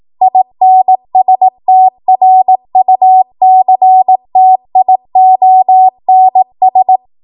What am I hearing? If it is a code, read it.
Morse code, heard as a keyed tone, about 18 wpm, 750 Hz. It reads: INSTRUCTIONS
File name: Archive / 18 WPM / 18WPM